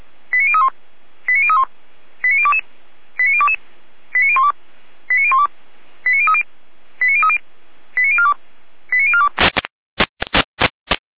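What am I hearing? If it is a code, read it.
Five-tone selective call (ZVEI), heard as quick runs of five tones, tones of about 70 ms. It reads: 89031 89031 89022 89022 89012 89012 89030 89030 89042 89042
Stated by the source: Five-tone ZVEI selective call received on an SDR.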